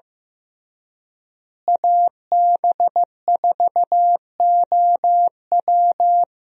Morse code, heard as a keyed tone, about 15 wpm, 700 Hz. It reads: AB4OW
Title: USA Call Signs - 5 Characters - Repeated 3x - 15wpm